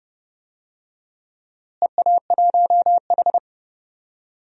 Morse code, heard as a keyed tone, about 30 wpm, 700 Hz. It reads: EA1H
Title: DXCC-100 Call Signs - 4 Characters - Repeated 3x - 30wpm